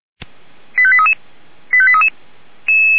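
Five-tone selective call (ZVEI), heard as quick runs of five tones, tones of about 70 ms. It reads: 85833 85833
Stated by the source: ZVEI five-tone call received with an SDR.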